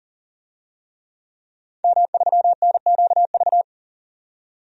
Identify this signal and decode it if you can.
Morse code, heard as a keyed tone, about 40 wpm, 700 Hz. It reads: M3NQV
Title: DXCC-100 Call Signs - 5 Characters - 40wpm